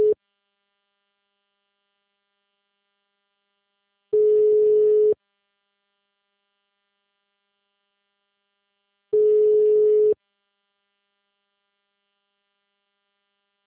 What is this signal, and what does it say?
Ringback tone (Europe, 425 Hz).
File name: out- -2013- - - .63378